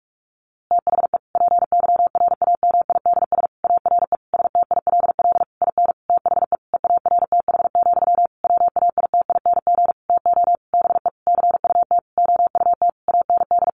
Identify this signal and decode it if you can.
Morse code, heard as a keyed tone, about 44 wpm, 700 Hz. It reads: THE PYRAMIDS ARE STILL IN THE EARTH, WAITING TO BE CUT OUT AND
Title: A Sound Of Thunder 45wpm